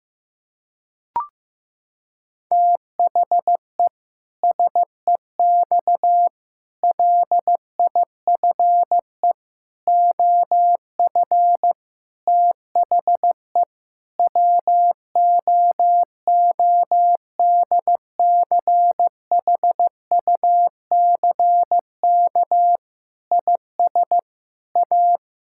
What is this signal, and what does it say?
Morse code, heard as a keyed tone, about 15 wpm, 700 Hz. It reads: THE SEX LIFE OF THE WOODCHUCK IS A
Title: Pangram Challenge v2 - 15wpm